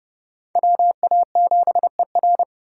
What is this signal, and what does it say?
Morse code, heard as a keyed tone, about 40 wpm, 700 Hz. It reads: WA7ER